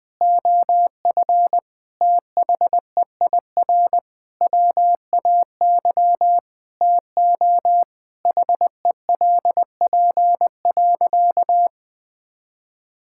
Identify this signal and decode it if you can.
Morse code, heard as a keyed tone, about 20 wpm, 700 Hz. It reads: OF THEIR WAY TO HELP.
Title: Sentences with 6-Word Phrases - 20wpm